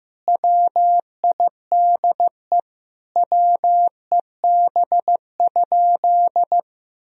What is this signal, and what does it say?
Morse code, heard as a keyed tone, about 15 wpm, 700 Hz. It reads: WIDE WEB?
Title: Sentences from Top 1000 Words - Review Part 4 - 15wpm